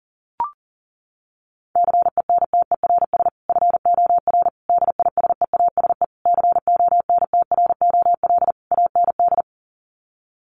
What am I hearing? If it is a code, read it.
Morse code, heard as a keyed tone, about 40 wpm, 700 Hz. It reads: CENTERS FOR DISEASE CONTROL AND
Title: Sets of 5 Words Encore2 40wpm